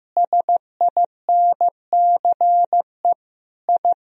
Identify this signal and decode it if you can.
Morse code, heard as a keyed tone, about 15 wpm, 700 Hz. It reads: SINCE I
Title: Sentences from Top 1000 Words - Review Part 1 - 15wpm